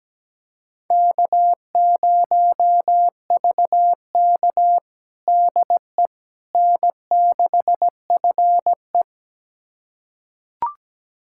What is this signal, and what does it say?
Morse code, heard as a keyed tone, about 17 wpm, 700 Hz. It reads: K0VK DE N6FE
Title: USA Callsign DE Callsign - 4 Characters - 17wpm